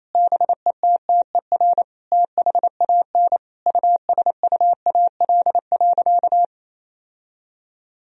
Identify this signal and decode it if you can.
Morse code, heard as a keyed tone, about 28 wpm, 700 Hz. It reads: BETTER THAN USUAL.
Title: Sentences from Top 1000 Words - Review Part 2 - 28wpm